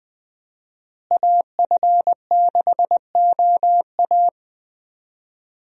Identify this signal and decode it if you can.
Morse code, heard as a keyed tone, about 20 wpm, 700 Hz. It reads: AF6OA
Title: USA Call Signs - 5 Characters - Repeated 2x - 20wpm